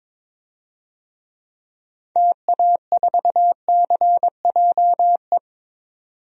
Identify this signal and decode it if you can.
Morse code, heard as a keyed tone, about 21 wpm, 700 Hz. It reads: TA4CJE
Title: DXCC-100 Call Signs - 6 Characters - 22wpm